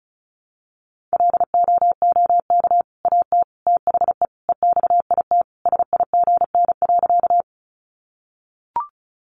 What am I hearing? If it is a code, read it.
Morse code, heard as a keyed tone, about 34 wpm, 700 Hz. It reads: LOOK AT THE EXIT SIGN.
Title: Sentences from Top 1100 Words 35wpm